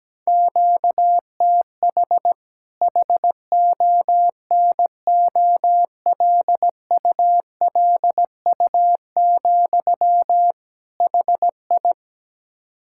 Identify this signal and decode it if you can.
Morse code, heard as a keyed tone, about 17 wpm, 700 Hz. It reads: QTH HONOLULU, HI
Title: K1USN SST QSO Practice 17wpm